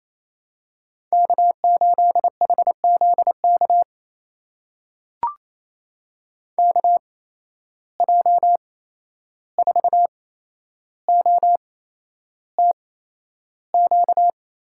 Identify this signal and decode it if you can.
Morse code, heard as a keyed tone, about 28 wpm, spaced as for 11 wpm, 700 Hz. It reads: K8HZK KJ4OTQ
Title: USA Call Signs - 8x Character Spacing - 28wpm